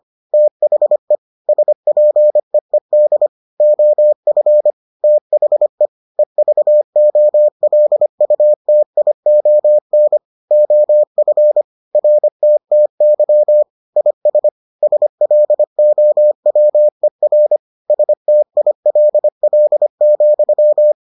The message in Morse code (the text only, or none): THE SPEED OF THE EVOLUTION OF RTTY IS SLOWER STILL,